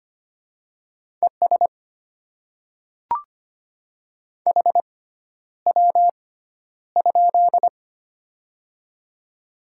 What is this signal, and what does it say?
Morse code, heard as a keyed tone, about 25 wpm, spaced as for 11 wpm, 700 Hz. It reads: ES HW?
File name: Top 20 QSO Elements - 6x Character Spacing - 25wpm